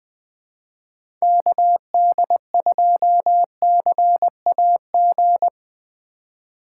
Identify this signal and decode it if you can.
Morse code, heard as a keyed tone, about 20 wpm, 700 Hz. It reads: KD2CAG
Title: USA Call Signs - 6 Characters - 3x Character Spacing - 20wpm